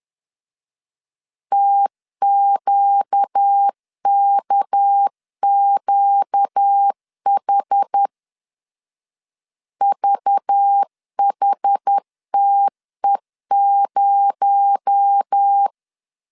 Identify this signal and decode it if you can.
Morse code, heard as a keyed tone, about 10 wpm, 800 Hz. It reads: TQKQH VHTE0
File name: CW / DET / 1 6 1500C S A 10wpm 4141